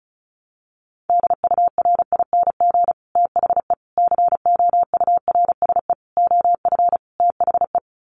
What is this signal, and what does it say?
Morse code, heard as a keyed tone, about 34 wpm, 700 Hz. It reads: DURING THE COURSE OF THE
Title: Sets of 5 Words Encore 35wpm